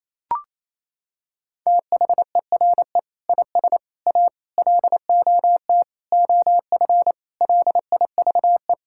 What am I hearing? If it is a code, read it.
Morse code, heard as a keyed tone, about 28 wpm, 700 Hz. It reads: THERE IS A LOT OF LIVE
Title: Sentences from Top 1000 Words - Review Part 5 - 28wpm